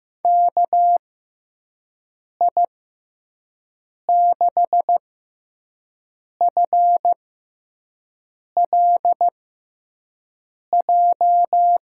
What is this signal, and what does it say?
Morse code, heard as a keyed tone, about 15 wpm, spaced as for 5 wpm, 700 Hz. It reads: KI6FLJ
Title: USA Call Signs - 6 Characters - 6x Character Spacing - 15wpm